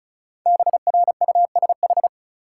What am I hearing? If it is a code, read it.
Morse code, heard as a keyed tone, about 34 wpm, 700 Hz. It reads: BRUSH
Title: Sentences from Top 1100 Words - New Word Drill Alt - 35wpm